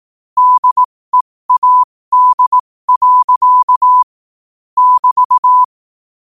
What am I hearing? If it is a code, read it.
Morse code, heard as a keyed tone, about 18 wpm, spaced as for 15 wpm, 1000 Hz. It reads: DEAD. <BT>
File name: qotdfiles / 15wpm